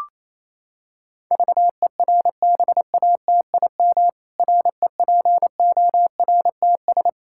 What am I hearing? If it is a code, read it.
Morse code, heard as a keyed tone, about 28 wpm, 700 Hz. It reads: VERBATIM REPORTS
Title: Pangram Challenge v2 - 28wpm